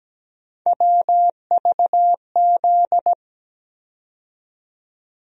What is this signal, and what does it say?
Morse code, heard as a keyed tone, about 17 wpm, 700 Hz. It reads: WVZ
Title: USA Call Sign Postfixes - 3 Letters 17wpm